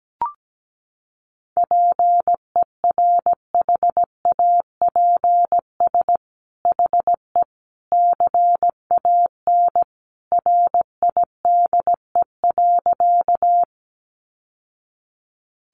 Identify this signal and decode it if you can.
Morse code, heard as a keyed tone, about 17 wpm, 700 Hz. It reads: PERHAPS HE CAN RIDE.
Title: Sentences from Top 1200 Words 17wpm